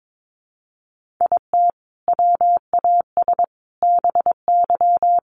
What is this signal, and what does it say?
Morse code, heard as a keyed tone, about 22 wpm, 700 Hz. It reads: IT WAS BY
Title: Sentences from Top 1500 Words 22wpm